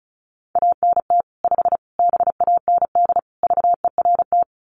Morse code, heard as a keyed tone, about 34 wpm, 700 Hz. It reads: ANT 5 BAND VERT
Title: Common QSO Phrases 35wpm